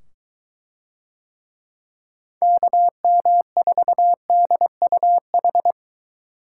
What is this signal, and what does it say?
Morse code, heard as a keyed tone, about 22 wpm, 700 Hz. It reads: KM4DUH